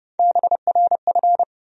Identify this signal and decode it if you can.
Morse code, heard as a keyed tone, about 32 wpm, 700 Hz. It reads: BRF
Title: USA Call Sign Postfixes - 3 Letters 30wpm